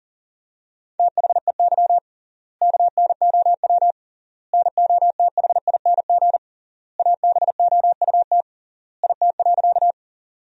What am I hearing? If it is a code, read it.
Morse code, heard as a keyed tone, about 40 wpm, 700 Hz. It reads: THEY KNOW NOTHING ABOUT IT.
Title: 3x Word-Spacing - Sentences from Top 500 Words - 40wpm